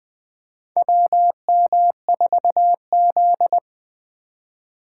Morse code, heard as a keyed tone, about 20 wpm, 700 Hz. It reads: WM4Z
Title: USA Call Signs - 4 Characters - 3x Character Spacing - 20wpm